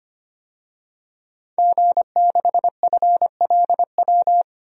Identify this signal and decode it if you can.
Morse code, heard as a keyed tone, about 25 wpm, 700 Hz. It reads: G6FLW
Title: DXCC-100 Call Signs - 5 Characters - 25wpm